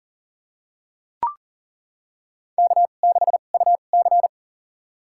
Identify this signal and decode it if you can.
Morse code, heard as a keyed tone, about 40 wpm, spaced as for 29 wpm, 700 Hz. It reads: K6UC
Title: USA Call Signs - 4 Characters - 2x Character Spacing - 40wpm